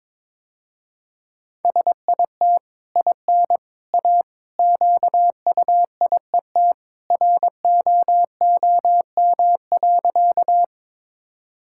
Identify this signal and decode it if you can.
Morse code, heard as a keyed tone, about 22 wpm, 700 Hz. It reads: SIT IN A QUIET ROOM.
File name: practice / 22wpm / Easy Sentences with Hints - 22wpm